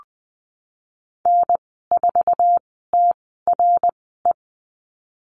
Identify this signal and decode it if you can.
Morse code, heard as a keyed tone, about 19 wpm, spaced as for 14 wpm, 700 Hz. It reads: N4TRE
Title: 055 Lesson 10g - Letter r - Call Signs 20wpm